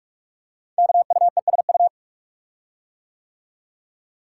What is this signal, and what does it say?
Morse code, heard as a keyed tone, about 46 wpm, 700 Hz. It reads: KUESU
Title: Code Groups - 5 Letter:Numbers - 45wpm